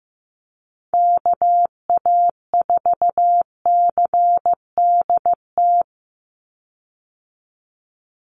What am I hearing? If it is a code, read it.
Morse code, heard as a keyed tone, about 15 wpm, 700 Hz. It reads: KA4CDT
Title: USA Call Signs 15wpm